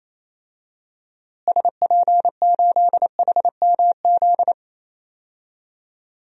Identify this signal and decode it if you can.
Morse code, heard as a keyed tone, about 28 wpm, 700 Hz. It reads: SP8HMZ
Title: DXCC-100 Call Signs - 6 Characters - Repeated 3x - 28wpm